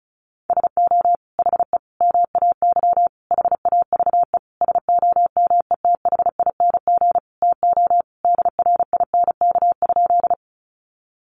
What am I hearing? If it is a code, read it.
Morse code, heard as a keyed tone, about 34 wpm, 700 Hz. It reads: SO HE MAY HAVE SOMETHING TO DRINK?
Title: Sentences from Top 1500 Words 35wpm